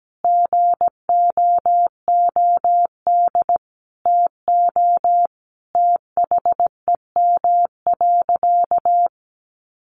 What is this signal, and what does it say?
Morse code, heard as a keyed tone, about 17 wpm, 700 Hz. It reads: GOOD TO THEM.
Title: Sentences from Top 1200 Words 17wpm